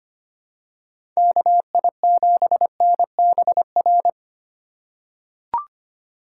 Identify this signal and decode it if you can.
Morse code, heard as a keyed tone, about 25 wpm, 700 Hz. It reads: KI7NBR
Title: USA Call Signs - 6 Characters - 2x Character Spacing - 25wpm